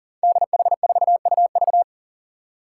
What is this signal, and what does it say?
Morse code, heard as a keyed tone, about 40 wpm, 700 Hz. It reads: DH4UV
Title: Five Challenging Letter-Number Combinations 40wpm